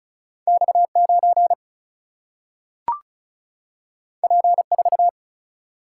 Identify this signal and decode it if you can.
Morse code, heard as a keyed tone, about 36 wpm, 700 Hz. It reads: X9 P4
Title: Single-Letter Single-Number 35wpm